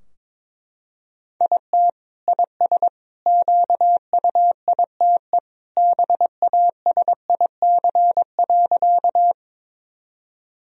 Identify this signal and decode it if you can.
Morse code, heard as a keyed tone, about 22 wpm, 700 Hz. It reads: IT IS QUITE BASIC.
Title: Sentences from Top 1000 Words - Review Part 3 - 22wpm